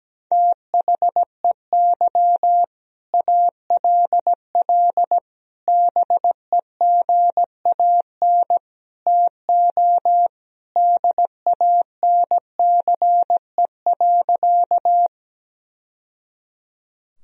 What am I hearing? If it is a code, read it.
Morse code, heard as a keyed tone, about 17 wpm, 700 Hz. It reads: THEY ALL BEGAN TO DANCE.
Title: Sentences from Top 1000 Words - Review Part 4 - 17wpm